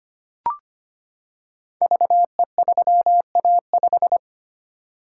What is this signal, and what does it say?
Morse code, heard as a keyed tone, about 25 wpm, 700 Hz. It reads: VE3A5
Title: Code Groups - 5 Letter:Numbers - 25wpm